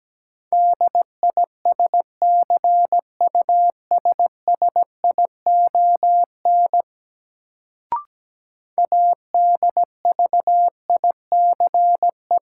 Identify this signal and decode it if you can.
Morse code, heard as a keyed tone, about 17 wpm, 700 Hz. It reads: DISCUSSION ADVICE
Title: Sentences from Top 1100 Words - New Word Drill - 17wpm